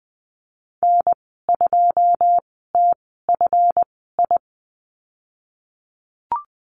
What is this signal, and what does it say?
Morse code, heard as a keyed tone, about 19 wpm, spaced as for 14 wpm, 700 Hz. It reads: N2TFI
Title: 167 Lesson 27g - Number 7 - Call Signs 20wpm